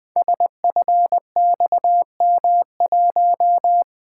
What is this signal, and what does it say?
Morse code, heard as a keyed tone, about 20 wpm, 700 Hz. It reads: SFXM1
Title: Code Groups - 5 Letter:Numbers - 20wpm